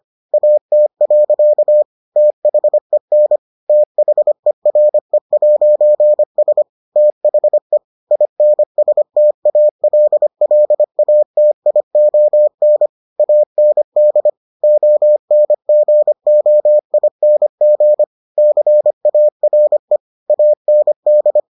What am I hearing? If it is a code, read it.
Morse code, heard as a keyed tone, about 25 wpm, 600 Hz. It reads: AT. THEN THERE'S THE INSTALLATION AND ONGOING CARE AND